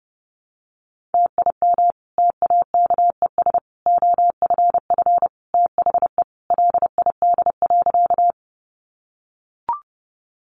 Morse code, heard as a keyed tone, about 30 wpm, 700 Hz. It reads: TIM TAKES OFF THE LID.
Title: Easy Phrases 30wpm